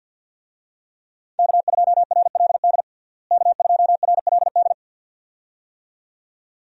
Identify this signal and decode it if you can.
Morse code, heard as a keyed tone, about 53 wpm, 700 Hz. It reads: K2RLD K2RLD
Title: USA Call Signs - 5 Characters - Repeated 2x - 50wpm